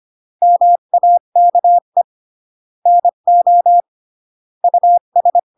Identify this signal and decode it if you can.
Morse code, heard as a keyed tone, about 25 wpm, 700 Hz. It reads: MAKE NO US
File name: CW / jodorowsky / 82 commandments 25-20